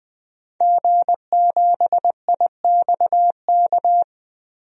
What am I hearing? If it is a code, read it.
Morse code, heard as a keyed tone, about 20 wpm, 700 Hz. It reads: G7IXK
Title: DXCC-100 Call Signs - 5 Characters - 20wpm